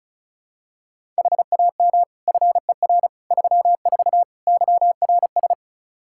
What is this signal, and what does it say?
Morse code, heard as a keyed tone, about 36 wpm, 700 Hz. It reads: HAM FER 34 YRS
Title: Practice QSO - Exchange 3 - More Info - 35wpm